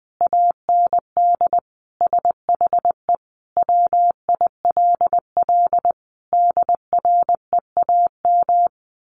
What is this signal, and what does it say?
Morse code, heard as a keyed tone, about 20 wpm, 700 Hz. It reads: AND SHE WILL DREAM
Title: Sentences from Top 1500 Words 20wpm